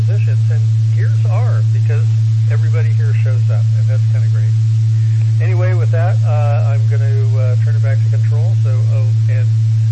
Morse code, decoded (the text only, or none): none